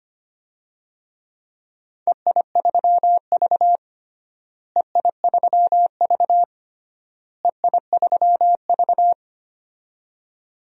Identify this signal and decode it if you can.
Morse code, heard as a keyed tone, about 25 wpm, 700 Hz. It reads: EI3V EI3V EI3V
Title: DXCC-100 Call Signs - 4 Characters - Repeated 3x - 25wpm